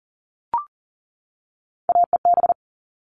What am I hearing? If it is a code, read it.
Morse code, heard as a keyed tone, about 40 wpm, 700 Hz. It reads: AEB
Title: All Cut Numbers - 3-Digit 40wpm